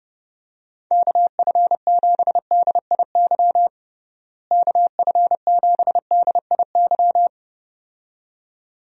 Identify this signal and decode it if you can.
Morse code, heard as a keyed tone, about 30 wpm, 700 Hz. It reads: KF7DIY KF7DIY
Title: USA Call Signs - 6 Characters - Repeated 2x - 30wpm